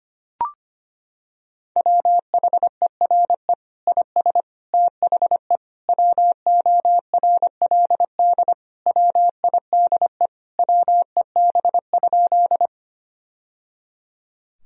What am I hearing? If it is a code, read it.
Morse code, heard as a keyed tone, about 25 wpm, 700 Hz. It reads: WHERE IS THE WORLD WIDE WEB?
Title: Sentences from Top 1000 Words - Review Part 4 - 25wpm